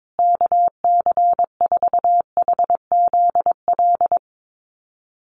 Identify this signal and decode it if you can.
Morse code, heard as a keyed tone, about 22 wpm, 700 Hz. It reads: KC4HZL
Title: USA Call Signs Repeated 3x 22wpm